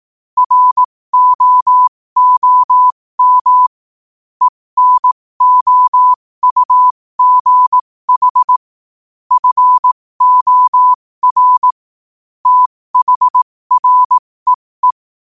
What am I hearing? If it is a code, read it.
Morse code, heard as a keyed tone, about 18 wpm, spaced as for 15 wpm, 1000 Hz. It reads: ROOM ENOUGH FOR THREE